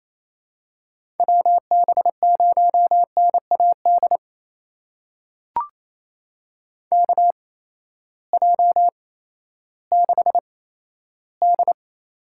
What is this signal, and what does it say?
Morse code, heard as a keyed tone, about 27 wpm, spaced as for 14 wpm, 700 Hz. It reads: WB0NAD KJ6D
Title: USA Call Signs - 6 Characters - 8x Character Spacing - 28wpm